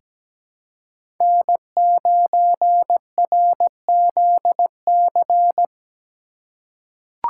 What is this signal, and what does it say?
Morse code, heard as a keyed tone, about 17 wpm, 700 Hz. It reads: N9RZC
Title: USA Call Signs - 5 Characters - 4x Character Spacing - 17wpm